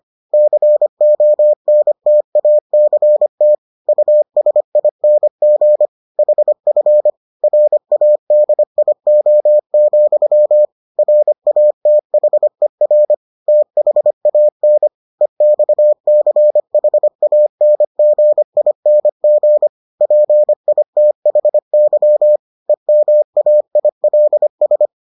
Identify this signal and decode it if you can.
Morse code, heard as a keyed tone, about 25 wpm, 600 Hz. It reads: CONTACT USING HF RADIO, RATHER THAN EXCHANGING PITHY EMAILS